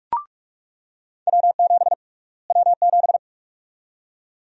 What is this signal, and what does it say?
Morse code, heard as a keyed tone, about 48 wpm, 700 Hz. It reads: W7 W7
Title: USA Call Sign Prefixes - Repeated 2x - 45wpm